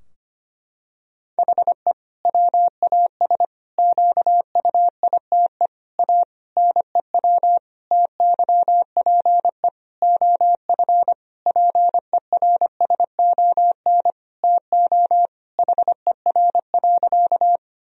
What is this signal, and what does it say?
Morse code, heard as a keyed tone, about 25 wpm, 700 Hz. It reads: HE WAS QUITE A NEW TYPE OF PERSON TO HER.
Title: Sentences from Top 1000 Words - Review Part 3 - 25wpm